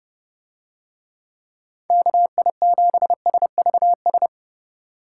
Morse code, heard as a keyed tone, about 31 wpm, 700 Hz. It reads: KI7SVS